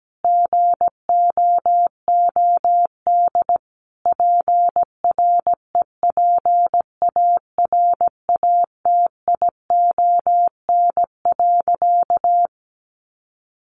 Morse code, heard as a keyed tone, about 17 wpm, 700 Hz. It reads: GOOD PREPARATION.
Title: Sentences from Top 1400 Words 17wpm